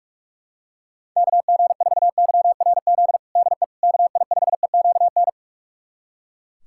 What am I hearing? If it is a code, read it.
Morse code, heard as a keyed tone, about 42 wpm, 700 Hz. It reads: KG4YRZ DE KI5EQN
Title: USA Callsign DE Callsign - 6 Characters - 45wpm